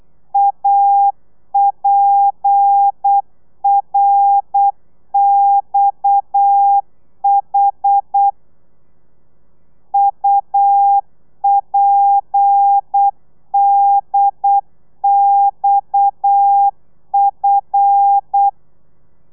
Morse code, heard as a keyed tone, about 8 wpm, 800 Hz. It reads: APRXH UPDXF